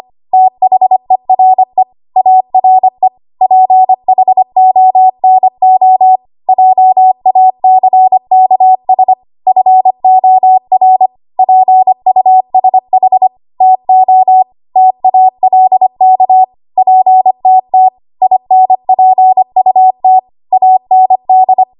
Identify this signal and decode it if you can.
Morse code, heard as a keyed tone, about 25 wpm, 750 Hz. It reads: THERE ARE PHONO JACKS FOR PUSH TO TALK PTT INPUT AND
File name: Archive / 25 WPM / 25WPM